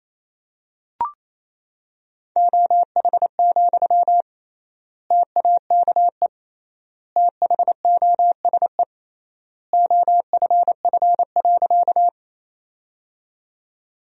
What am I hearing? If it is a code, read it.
Morse code, heard as a keyed tone, about 28 wpm, 700 Hz. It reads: OH, TAKE THOSE OFF.
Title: 3x Word-Spacing - Sentences from Top 500 Words - 28wpm